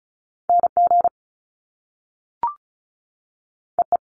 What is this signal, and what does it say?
Morse code, heard as a keyed tone, about 35 wpm, 700 Hz. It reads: N G E E
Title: Top Two Letter Combinations in Words 35wpm